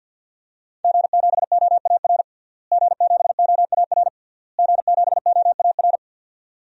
Morse code, heard as a keyed tone, about 51 wpm, 700 Hz. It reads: G7OAR G7OAR G7OAR
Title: DXCC-100 Call Signs - 5 Characters - Repeated 3x - 50wpm